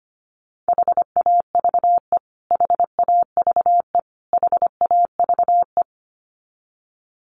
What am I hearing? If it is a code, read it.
Morse code, heard as a keyed tone, about 25 wpm, 700 Hz. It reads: HAVE HAVE HAVE
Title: Top 200 Words Repeated 25wpm